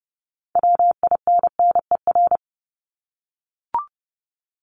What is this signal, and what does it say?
Morse code, heard as a keyed tone, about 30 wpm, 700 Hz. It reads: WINNER
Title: Top 1751-2000 Words 30wpm